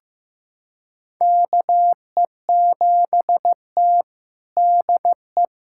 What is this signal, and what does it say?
Morse code, heard as a keyed tone, about 15 wpm, 700 Hz. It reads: KE7T DE
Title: USA Callsign DE Callsign - 4 Characters - 15wpm